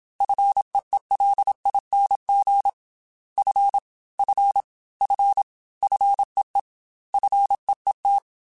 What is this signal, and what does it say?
Morse code, heard as a keyed tone, about 27 wpm, 800 Hz. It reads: FEELING F F F FEE FEET
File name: Single F mp3s / Single F Prefixes 3x2 Twice @25